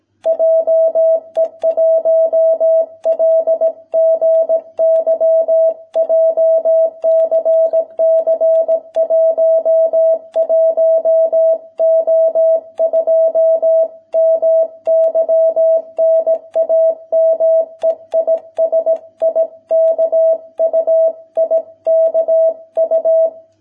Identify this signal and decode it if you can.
Morse code, heard as a keyed tone, about 17 wpm, 650 Hz. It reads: JE1LGY JCC 11O2 MY NAME IS IKU IKU